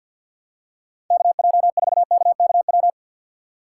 Morse code, heard as a keyed tone, about 51 wpm, 690 Hz. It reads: KJ4KKW
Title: USA Call Signs - 6 Characters - 3x Character Spacing - 50wpm